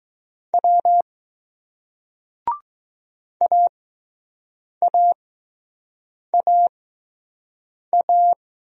Morse code, heard as a keyed tone, about 20 wpm, spaced as for 6 wpm, 700 Hz. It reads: W AAAA